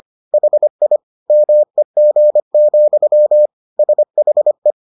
Morse code, heard as a keyed tone, about 25 wpm, 600 Hz. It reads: HI MEG, SHE